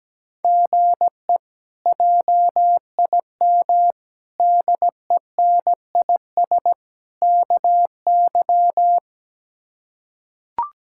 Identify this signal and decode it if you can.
Morse code, heard as a keyed tone, about 17 wpm, 700 Hz. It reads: GE JIM DENIS KY
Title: K1USN SST QSO Practice - 17wpm